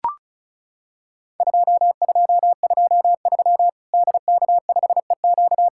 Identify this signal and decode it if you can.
Morse code, heard as a keyed tone, about 36 wpm, 700 Hz. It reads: 2223 DK5EQ
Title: Worked All Europe DX Contest - QTC Exchange - 35wpm